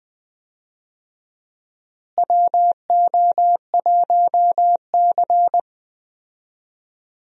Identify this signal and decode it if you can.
Morse code, heard as a keyed tone, about 20 wpm, 700 Hz. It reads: WO1C
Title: USA Call Signs - 4 Characters - Repeated 3x - 20wpm